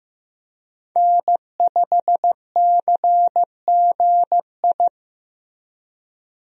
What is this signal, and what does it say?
Morse code, heard as a keyed tone, about 15 wpm, 700 Hz. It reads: N5CGI
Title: USA Call Signs - 5 Characters - Repeated 2x - 15wpm